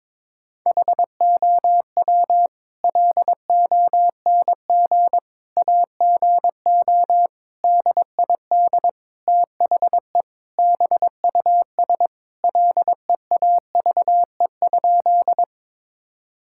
Morse code, heard as a keyed tone, about 22 wpm, 700 Hz. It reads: HOW LONG AGO DID THE BUS LEAVE?
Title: Sentences from Top 1000 Words - Review Part 6 - 22wpm